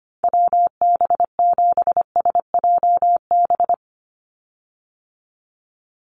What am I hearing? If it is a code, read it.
Morse code, heard as a keyed tone, about 25 wpm, 700 Hz. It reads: WB7SJB
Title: USA Call Signs 25wpm